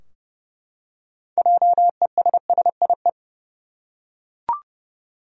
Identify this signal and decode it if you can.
Morse code, heard as a keyed tone, about 30 wpm, 700 Hz. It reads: JESSIE
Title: USA Common Names - 30wpm